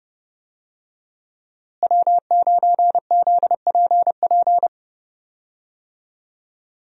Morse code, heard as a keyed tone, about 30 wpm, 700 Hz. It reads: W9ZPP
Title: USA Call Signs - 5 Characters - Repeated 3x - 30wpm